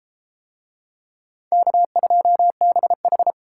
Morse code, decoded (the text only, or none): K2BH